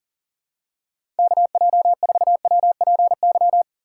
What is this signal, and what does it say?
Morse code, heard as a keyed tone, about 40 wpm, 700 Hz. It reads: KJ4WPY
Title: USA Call Signs - 6 Characters - 6x Character Spacing - 40wpm